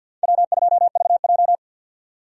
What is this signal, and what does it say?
Morse code, heard as a keyed tone, about 52 wpm, 700 Hz. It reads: W2VJ
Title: USA Call Signs - 4 Characters - 50wpm